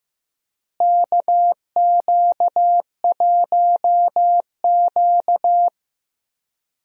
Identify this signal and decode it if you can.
Morse code, heard as a keyed tone, about 15 wpm, 700 Hz. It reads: KQ1Q
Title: USA Call Signs - 4 Characters - Repeated 3x - 15wpm